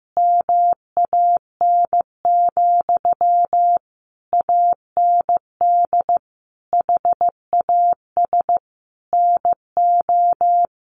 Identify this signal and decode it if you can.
Morse code, heard as a keyed tone, about 15 wpm, 700 Hz. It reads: MAN, AND HAS NO
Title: Sentences from Top 1100 Words 15wpm